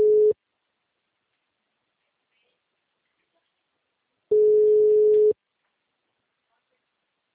Ringback tone (Europe, 425 Hz).